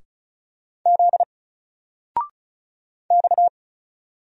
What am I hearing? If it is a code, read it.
Morse code, heard as a keyed tone, about 35 wpm, 700 Hz. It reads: Z X
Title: Single Letter-Number 35wpm